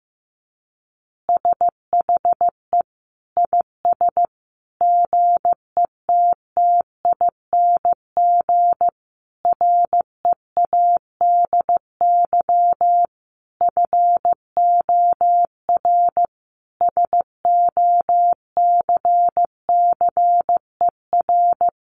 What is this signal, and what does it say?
Morse code, heard as a keyed tone, about 15 wpm, 700 Hz. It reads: SHE IS GETTING READY FOR SOCCER
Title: Easy Phrases 15wpm